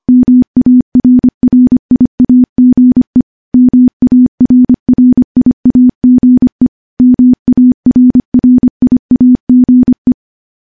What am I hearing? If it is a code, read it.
Morse code, heard as a keyed tone, about 25 wpm, 250 Hz. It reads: MARRIAGE MARRIAGE MARRIAGE
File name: morse-training-25WPM